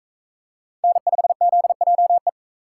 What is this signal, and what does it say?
Morse code, heard as a keyed tone, about 40 wpm, 700 Hz. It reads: N5ZJE